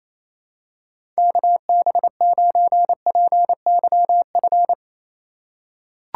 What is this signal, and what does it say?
Morse code, heard as a keyed tone, about 28 wpm, 700 Hz. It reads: KB9PYF
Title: USA Call Signs - 6 Characters - 2x Character Spacing - 28wpm